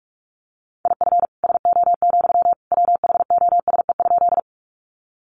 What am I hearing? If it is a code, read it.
Morse code, heard as a keyed tone, about 44 wpm, 700 Hz. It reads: IF SO, WHOSE?
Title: Sentences from Top 1200 Words 45wpm